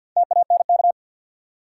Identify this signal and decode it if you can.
Morse code, heard as a keyed tone, about 48 wpm, 700 Hz. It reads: TANK
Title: Four Letter Words - 50wpm